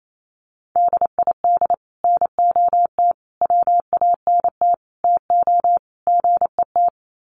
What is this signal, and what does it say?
Morse code, heard as a keyed tone, about 28 wpm, 700 Hz. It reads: DID NOT WANT TO GET
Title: Sets of 5 Words Encore2 28wpm